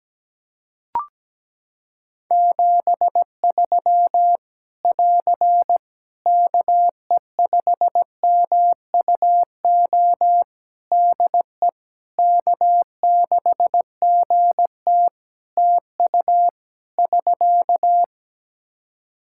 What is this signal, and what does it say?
Morse code, heard as a keyed tone, about 17 wpm, 700 Hz. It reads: 73 <AR> KE5MUO DE K6GT TU <SK>
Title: Practice QSO - Exchange 4 - Ending - 17wpm